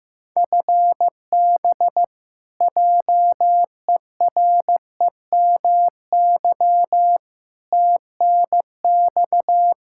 Morse code, heard as a keyed tone, about 15 wpm, 700 Hz. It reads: FB JEREMY TNX
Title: Practice QSO - Exchange 3 - More Info - 15wpm